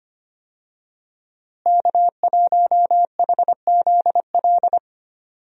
Morse code, heard as a keyed tone, about 25 wpm, 700 Hz. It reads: K1HZL
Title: USA Call Signs - 5 Characters - Repeated 3x - 25wpm